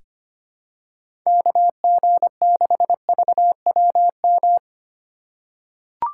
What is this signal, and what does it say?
Morse code, heard as a keyed tone, about 25 wpm, 690 Hz. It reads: KG6VWM
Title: USA Call Signs - 6 Characters - 3x Character Spacing - 25wpm